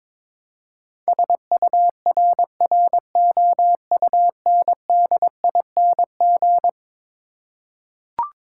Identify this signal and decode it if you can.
Morse code, heard as a keyed tone, about 22 wpm, 700 Hz. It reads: SURROUNDING
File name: practice / 22wpm / Eleven Letter Words - 22wpm